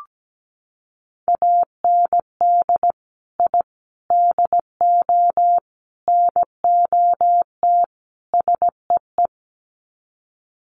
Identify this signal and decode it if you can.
Morse code, heard as a keyed tone, about 17 wpm, 700 Hz. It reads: AND I DO NOT SEE
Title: Sets of 5 Words Encore2 17wpm